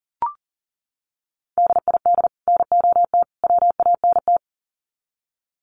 Morse code, heard as a keyed tone, about 40 wpm, 700 Hz. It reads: DID NOT WANT
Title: Sets of 3 Words 40wpm